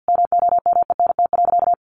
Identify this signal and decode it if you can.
Morse code, heard as a keyed tone, about 50 wpm, 700 Hz. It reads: MOMENT.
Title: Sentences from Top 1000 Words 50wpm